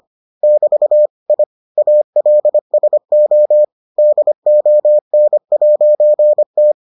Morse code, heard as a keyed tone, about 25 wpm, 600 Hz. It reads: <BT> I ALSO DON'T